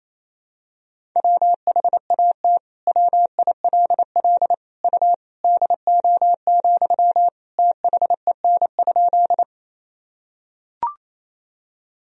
Morse code, heard as a keyed tone, about 28 wpm, 700 Hz. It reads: WHAT WILL U DO, THEN?
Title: Sentences from Top 100 Words and Abbreviations 28wpm